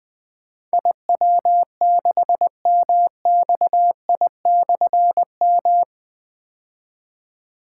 Morse code, heard as a keyed tone, about 20 wpm, 700 Hz. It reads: IW6MXI/M
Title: Difficult Call Signs 20wpm